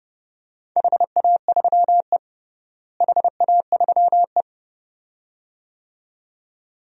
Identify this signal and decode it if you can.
Morse code, heard as a keyed tone, about 32 wpm, 700 Hz. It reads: HA3E HA3E
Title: DXCC-100 Call Signs - 4 Characters - Repeated 2x - 30wpm